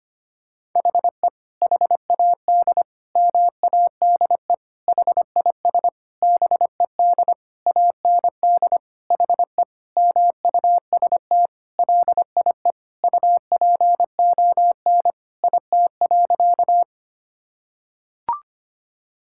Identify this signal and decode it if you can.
Morse code, heard as a keyed tone, about 25 wpm, 700 Hz. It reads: HE HAD MADE HIS BED AND HE MUST LIE UPON IT.
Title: Sentences from Top 1600 Words 25wpm